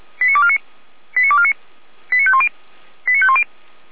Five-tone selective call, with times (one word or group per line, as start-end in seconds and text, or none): 0.2-0.6 89249
1.2-1.5 89249
2.1-2.5 89510
3.1-3.4 89510